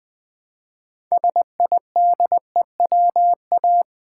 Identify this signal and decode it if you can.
Morse code, heard as a keyed tone, about 20 wpm, 700 Hz. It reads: SIDEWA
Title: Build A Word - 3 Letters At A Time - 8-Letter Words - 20wpm